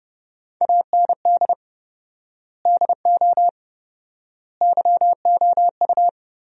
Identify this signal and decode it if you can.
Morse code, heard as a keyed tone, about 30 wpm, 700 Hz. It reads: AND DO YOU
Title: 4x Word-Spacing - Sentences from Top 100 Words - 30wpm